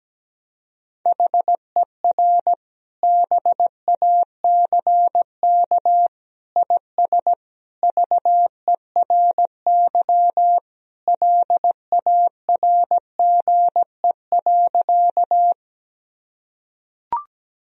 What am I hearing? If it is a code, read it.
Morse code, heard as a keyed tone, about 17 wpm, 700 Hz. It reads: HER BACK IS VERY LARGE.
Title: Sentences from Top 1000 Words - Review Part 1 - 17wpm